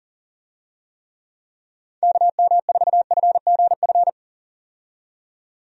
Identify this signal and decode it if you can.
Morse code, heard as a keyed tone, about 40 wpm, 700 Hz. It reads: KM4FGF
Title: USA Call Signs - 6 Characters - 40wpm